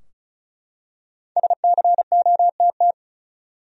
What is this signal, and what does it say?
Morse code, heard as a keyed tone, about 35 wpm, 700 Hz. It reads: SCOTT
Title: USA Common Names - Max 5 Letters - 35wpm